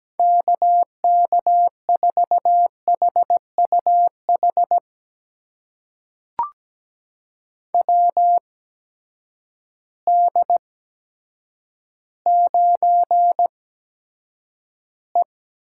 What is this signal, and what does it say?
Morse code, heard as a keyed tone, about 17 wpm, spaced as for 8 wpm, 700 Hz. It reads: KK4HUH WD9E
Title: USA Call Signs - 6 Characters - 8x Character Spacing - 17wpm